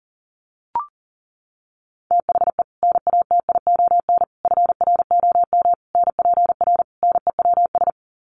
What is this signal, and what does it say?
Morse code, heard as a keyed tone, about 40 wpm, 700 Hz. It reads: THE NATION FROM NPR NEWS
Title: Sets of 5 Words 40wpm